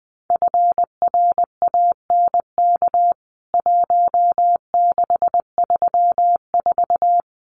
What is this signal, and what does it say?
Morse code, heard as a keyed tone, about 20 wpm, 700 Hz. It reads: FRANK 1634
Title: CWT Exchange 20wpm